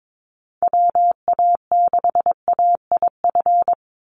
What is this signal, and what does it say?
Morse code, heard as a keyed tone, about 22 wpm, 700 Hz. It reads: WA6AIF
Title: USA Call Signs 22wpm